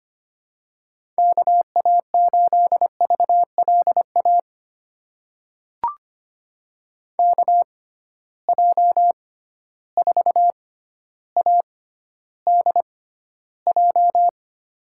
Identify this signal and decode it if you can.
Morse code, heard as a keyed tone, about 25 wpm, spaced as for 13 wpm, 700 Hz. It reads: KA8VLA KJ4ADJ
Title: USA Call Signs - 6 Characters - 6x Character Spacing - 25wpm